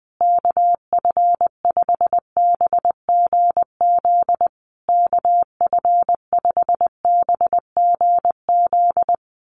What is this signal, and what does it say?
Morse code, heard as a keyed tone, about 20 wpm, 700 Hz. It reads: KF5BGZ KF5BGZ
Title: USA Call Signs Repeated 3x 20wpm